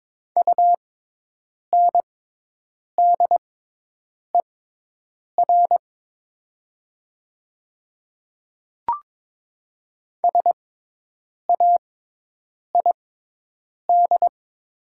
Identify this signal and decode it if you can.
Morse code, heard as a keyed tone, about 22 wpm, spaced as for 8 wpm, 690 Hz. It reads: UNDER SAID
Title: 6x Character Spacing - Top 500 Words - 22wpm